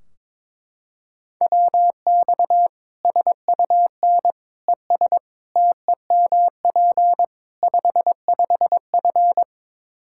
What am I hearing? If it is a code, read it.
Morse code, heard as a keyed tone, about 22 wpm, 700 Hz. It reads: WX SUN ES TEMP 55F
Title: K1USN SST QSO Practice 22wpm